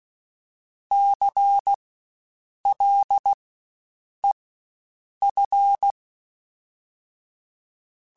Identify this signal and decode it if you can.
Morse code, heard as a keyed tone, about 16 wpm, spaced as for 7 wpm, 800 Hz. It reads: CLEF
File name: Spaced Alphas / Spaced Alphas mp3s / Cs sp @15